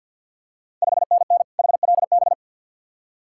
Morse code, heard as a keyed tone, about 52 wpm, 700 Hz. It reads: 5NN HLB
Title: Ontario QSO Party Exchange 50wpm